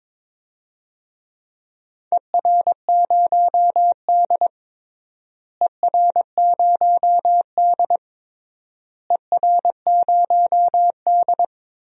Morse code, heard as a keyed tone, about 21 wpm, 700 Hz. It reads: ER0D ER0D ER0D
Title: DXCC-100 Call Signs - 4 Characters - Repeated 3x - 22wpm